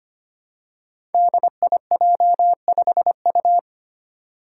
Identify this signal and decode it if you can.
Morse code, heard as a keyed tone, about 25 wpm, 700 Hz. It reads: DIJ5U
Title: Five Challenging Letter-Number Combinations 25wpm